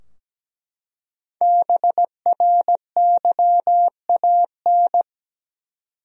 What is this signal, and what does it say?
Morse code, heard as a keyed tone, about 17 wpm, 700 Hz. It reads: BRYAN
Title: USA Common Names - Max 5 Letters - 17wpm